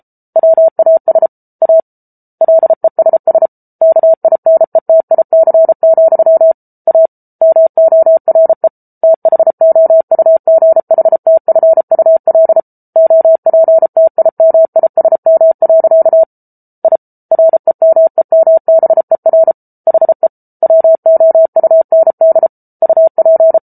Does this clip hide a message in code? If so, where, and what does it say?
Yes: Morse code from 0.4 to 23.6 s, heard as a keyed tone, about 34 wpm, 650 Hz. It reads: WAS A LESS KINETIC, A MORE THOUGHTFUL OPTIMISM. I REMEMBER HE WOUND UP